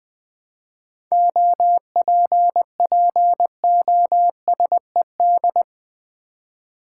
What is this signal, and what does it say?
Morse code, heard as a keyed tone, about 20 wpm, 700 Hz. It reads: OPPOSED
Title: 7 Letter Words - First 3 Letters Given - 20wpm